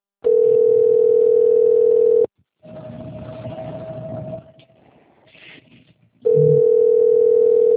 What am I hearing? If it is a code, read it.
Ringback tone (North America), 440 + 480 Hz.